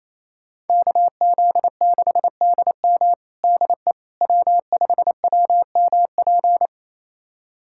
Morse code, heard as a keyed tone, about 28 wpm, 700 Hz. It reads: KZ6DM DE W5WMP
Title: USA Callsign DE Callsign - 5 Characters - 28wpm